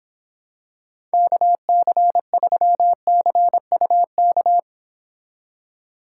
Morse code, heard as a keyed tone, about 26 wpm, 700 Hz. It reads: KC3CUK